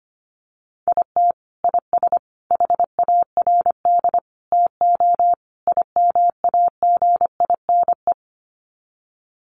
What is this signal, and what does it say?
Morse code, heard as a keyed tone, about 25 wpm, 700 Hz. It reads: IT IS HARD TO IMAGINE
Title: Sets of 5 Words Encore2 25wpm